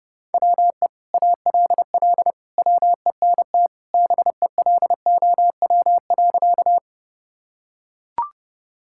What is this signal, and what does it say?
Morse code, heard as a keyed tone, about 30 wpm, 700 Hz. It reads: WE ALL WENT BELOW.
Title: Sentences from Top 1000 Words - Review Part 5 - 30wpm